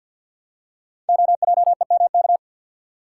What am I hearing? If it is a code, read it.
Morse code, heard as a keyed tone, about 50 wpm, 700 Hz. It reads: OJEMK
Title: Code Groups - 5 Letters - 50wpm